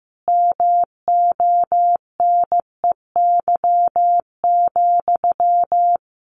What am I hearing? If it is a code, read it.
Morse code, heard as a keyed tone, about 15 wpm, 700 Hz. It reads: MONEY,
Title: Sentences from Top 900 Words 15wpm